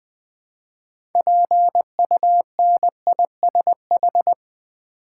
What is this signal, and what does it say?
Morse code, heard as a keyed tone, about 20 wpm, 700 Hz. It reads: PUNISH
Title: Build A Word - 2 Letters At A Time - 6-Letter Words - 20wpm